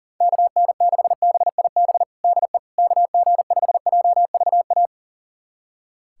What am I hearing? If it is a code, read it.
Morse code, heard as a keyed tone, about 37 wpm, 700 Hz. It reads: KN6BIB DE KG5JVA
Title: USA Callsign DE Callsign - 6 Characters - 40wpm